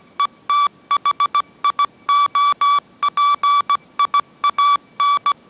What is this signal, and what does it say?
Morse code, heard as a keyed tone, about 17 wpm, 1150 Hz. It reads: ETHIOPIAN
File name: mrx / mrx080 18wpm